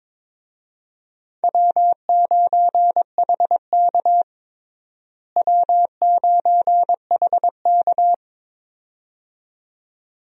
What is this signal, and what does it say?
Morse code, heard as a keyed tone, about 22 wpm, 700 Hz. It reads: W9HK W9HK
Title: USA Call Signs - 4 Characters - Repeated 2x - 22wpm